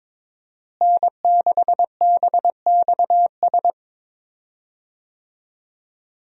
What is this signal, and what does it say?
Morse code, heard as a keyed tone, about 22 wpm, 700 Hz. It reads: N6BXS
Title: USA Call Signs - 5 Characters - Repeated 3x - 22wpm